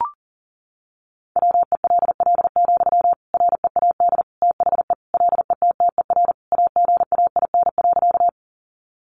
Morse code, heard as a keyed tone, about 40 wpm, 700 Hz. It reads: WELL, READ THE LETTER AGAIN.
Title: Sentences from Top 300 Words 40wpm